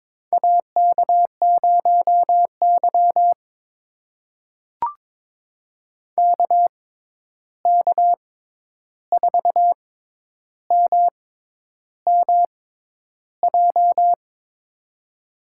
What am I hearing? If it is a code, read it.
Morse code, heard as a keyed tone, about 22 wpm, spaced as for 10 wpm, 700 Hz. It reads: AK0Y KK4MMJ
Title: USA Call Signs - 6x Character Spacing - 22wpm